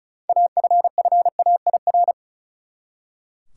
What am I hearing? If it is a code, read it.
Morse code, heard as a keyed tone, about 36 wpm, 700 Hz. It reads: AFFAIR
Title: Six Letter Words - 35wpm